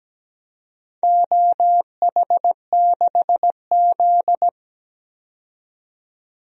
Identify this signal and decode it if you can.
Morse code, heard as a keyed tone, about 17 wpm, 700 Hz. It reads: OH6Z
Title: DXCC-100 Call Signs - 4 Characters - 17wpm